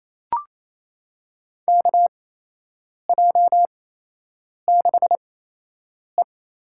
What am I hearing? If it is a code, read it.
Morse code, heard as a keyed tone, about 27 wpm, spaced as for 8 wpm, 700 Hz. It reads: KJ6E
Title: USA Call Signs - 6 Characters - 8x Character Spacing - 28wpm